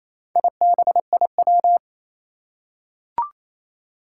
Morse code, heard as a keyed tone, about 27 wpm, 700 Hz. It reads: IBIW
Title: Four Challenging Letter-Number Combinations 28wpm